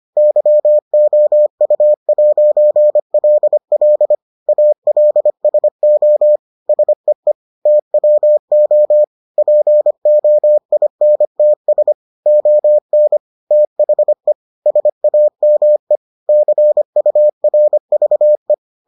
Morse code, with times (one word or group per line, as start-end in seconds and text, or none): 0.2-4.1 YOU'LL
4.5-6.4 ALSO
6.7-7.3 SEE
7.7-9.0 TWO
9.4-11.9 POINTS
12.3-13.2 ON
13.5-14.3 THE
14.7-16.0 SAME
16.3-18.5 CURVE